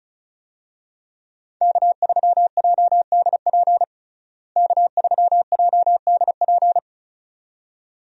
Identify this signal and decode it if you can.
Morse code, heard as a keyed tone, about 34 wpm, 700 Hz. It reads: K3JDP K3JDP
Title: USA Call Signs - 5 Characters - Repeated 2x - 35wpm